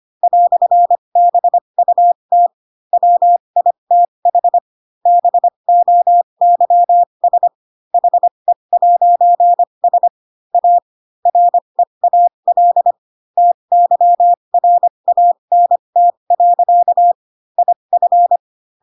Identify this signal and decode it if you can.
Morse code, heard as a keyed tone, about 25 wpm, spaced as for 22 wpm, 700 Hz. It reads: "BUT WITH BOYS HE'S A REAL TYRANT. IF
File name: Pinocchio / Pinocchio ch3 18 25 ik1hge